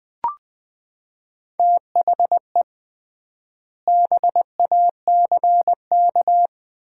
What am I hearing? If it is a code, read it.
Morse code, heard as a keyed tone, about 21 wpm, 700 Hz. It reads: THE BACK
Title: 3x Word-Spacing - Sets of 3 Words - 20wpm